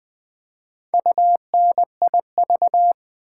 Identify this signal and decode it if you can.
Morse code, heard as a keyed tone, about 20 wpm, 700 Hz. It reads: UNIV
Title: Build A Word - 4 Letters At A Time - 8-Letter Words - 20wpm